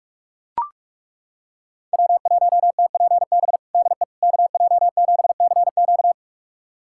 Morse code, heard as a keyed tone, about 43 wpm, 700 Hz. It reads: W1TPB DE KJ7CQ
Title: USA Callsign DE Callsign - 5 Characters - 45wpm